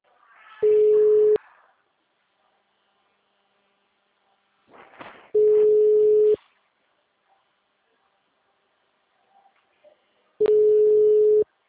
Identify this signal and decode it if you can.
Ringback tone (Europe, 425 Hz).